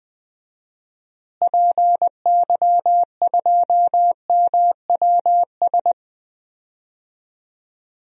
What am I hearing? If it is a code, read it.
Morse code, heard as a keyed tone, about 21 wpm, 700 Hz. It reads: PY2MWS
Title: DXCC-100 Call Signs - 6 Characters - Repeated 2x - 20wpm